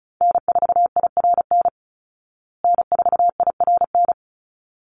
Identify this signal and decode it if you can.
Morse code, heard as a keyed tone, about 34 wpm, 700 Hz. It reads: N4IRN N4IRN
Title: Call Signs Repeated 3x - 35wpm